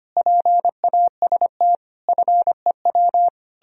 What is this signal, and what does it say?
Morse code, heard as a keyed tone, about 32 wpm, 700 Hz. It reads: PAST FEW